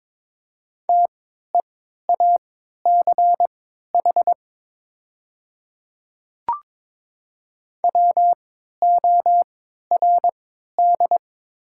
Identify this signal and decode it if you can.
Morse code, heard as a keyed tone, about 22 wpm, spaced as for 13 wpm, 700 Hz. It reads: TEACH WORD
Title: 3x Character Spacing - Top 500 Words - 22wpm